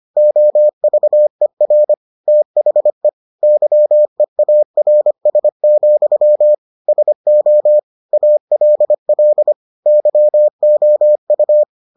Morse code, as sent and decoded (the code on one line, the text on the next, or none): --- ...- . .-. / - .... / -.-- . .- .-. ... --..-- / ... --- / .- .-.. .-.. / -.-- --- ..-
OVER THE YEARS, SO ALL YOU